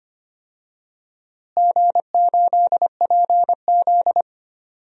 Morse code, heard as a keyed tone, about 25 wpm, 700 Hz. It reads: G8PZ
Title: DXCC-100 Call Signs - 4 Characters - Repeated 2x - 25wpm